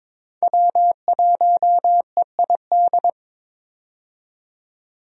Morse code, heard as a keyed tone, about 22 wpm, 700 Hz. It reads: W1EID
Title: USA Call Signs - 5 Characters - 22wpm